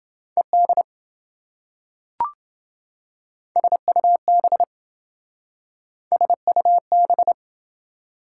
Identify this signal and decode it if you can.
Morse code, heard as a keyed tone, about 27 wpm, 700 Hz. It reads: ED SUB SUB